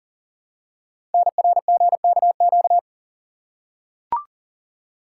Morse code, heard as a keyed tone, about 41 wpm, 700 Hz. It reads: NRGKQ
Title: Code Groups - 5 Letters - 40wpm